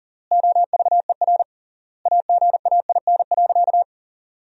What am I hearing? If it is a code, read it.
Morse code, heard as a keyed tone, about 40 wpm, 700 Hz. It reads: OVER AGAIN.
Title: 3x Word-Spacing - Sentences from Top 500 Words - 40wpm